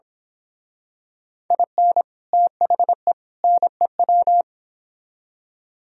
Morse code, heard as a keyed tone, about 27 wpm, 700 Hz. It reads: IN THE NEW
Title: Speed-Racing - Sets of 3 Words 22wpm